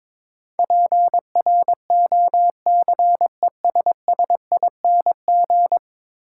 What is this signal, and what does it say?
Morse code, heard as a keyed tone, about 22 wpm, 700 Hz. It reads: PROCESSING
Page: morse-code-ninja-practice-22wpm